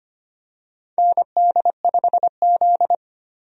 Morse code, heard as a keyed tone, about 25 wpm, 700 Hz. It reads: ND5Z
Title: USA Call Signs - 4 Characters - Repeated 2x - 25wpm